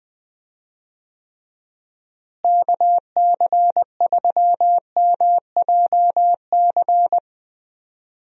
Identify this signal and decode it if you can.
Morse code, heard as a keyed tone, about 20 wpm, 700 Hz. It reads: KC3MJC
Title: USA Call Signs - 6 Characters - Repeated 3x - 20wpm